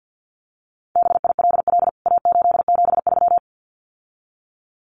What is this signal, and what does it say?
Morse code, heard as a keyed tone, about 48 wpm, 700 Hz. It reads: BILL A873
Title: CWT Exchange 50wpm